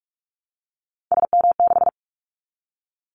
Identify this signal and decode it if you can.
Morse code, heard as a keyed tone, about 46 wpm, 700 Hz. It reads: SM6
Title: Callsign Prefixes for Top 25 DXCC - 45wpm